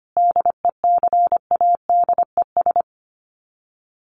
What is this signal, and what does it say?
Morse code, heard as a keyed tone, about 25 wpm, 700 Hz. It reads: DECADES
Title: Sets of 5 Words Encore2 25wpm